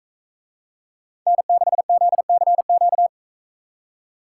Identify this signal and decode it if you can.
Morse code, heard as a keyed tone, about 41 wpm, 700 Hz. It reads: N6ZCQ